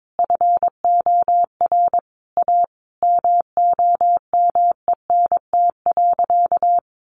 Morse code, heard as a keyed tone, about 22 wpm, 700 Hz. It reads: FOR A MOMENT.
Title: Sentences from Top 1000 Words 22wpm